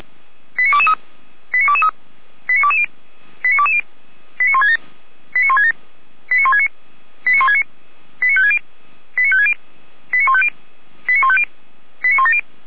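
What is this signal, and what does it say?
Five-tone selective call (ZVEI), heard as quick runs of five tones, tones of about 80 ms. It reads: 89223 89223 89220 89220 89167 89167 89169 89169 89560 89560 89150 89150 89170
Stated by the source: ZVEI five-tone selective call received by SDR, code 89163.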